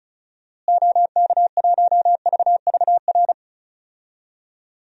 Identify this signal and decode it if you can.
Morse code, heard as a keyed tone, about 34 wpm, 700 Hz. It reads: OK1VVR
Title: DXCC-100 Call Signs - 6 Characters - Repeated 3x - 35wpm